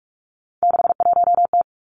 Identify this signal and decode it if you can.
Morse code, heard as a keyed tone, about 46 wpm, 700 Hz. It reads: 61T
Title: Three Challenging Letter-Number Combinations 45wpm